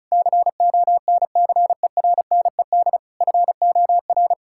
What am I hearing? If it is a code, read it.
Morse code, heard as a keyed tone, about 34 wpm, 700 Hz. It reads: CONCERNED FOR
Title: Sentences from Top 1000 Words - Review Part 6 - 35wpm